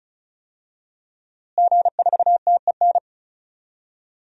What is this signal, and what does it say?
Morse code, heard as a keyed tone, about 36 wpm, 700 Hz. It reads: G4TEN
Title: DXCC-100 Call Signs - 5 Characters - Repeated 2x - 35wpm